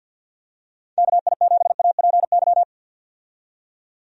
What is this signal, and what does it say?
Morse code, heard as a keyed tone, about 51 wpm, 700 Hz. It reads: KI7APY
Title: USA Call Signs - 6 Characters - 50wpm